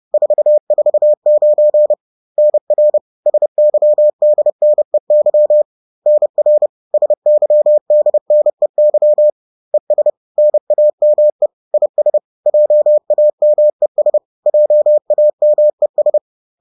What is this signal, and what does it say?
Morse code, heard as a keyed tone, about 30 wpm, 600 Hz. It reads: 449 NR SYDNEY NR SYDNEY ES NAME IS JAMES JAMES